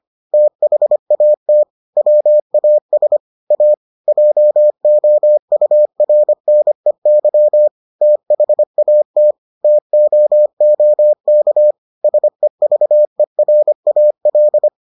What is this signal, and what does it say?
Morse code, heard as a keyed tone, about 25 wpm, 600 Hz. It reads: THAT WAS A JOURNEY THAT TOOK SEVERAL